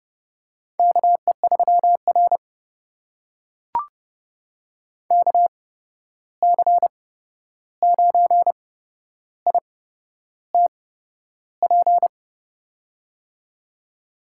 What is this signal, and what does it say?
Morse code, heard as a keyed tone, about 31 wpm, spaced as for 11 wpm, 700 Hz. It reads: KE3R KC9ITP
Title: USA Call Signs - 8x Character Spacing - 30wpm